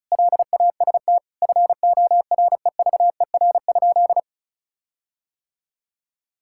Morse code, heard as a keyed tone, about 36 wpm, 700 Hz. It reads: LAST FOREVER?
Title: Sentences from Top 1600 Words 35wpm